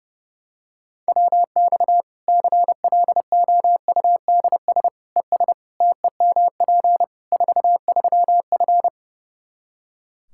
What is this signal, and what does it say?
Morse code, heard as a keyed tone, about 30 wpm, 700 Hz. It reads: WX CLOUDS ES TEMP 43F
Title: K1USN SST QSO Practice 30wpm